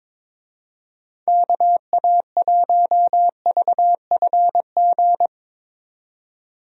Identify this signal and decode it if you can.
Morse code, heard as a keyed tone, about 22 wpm, 700 Hz. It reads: KA1VFG
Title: USA Call Signs - 6 Characters - 22wpm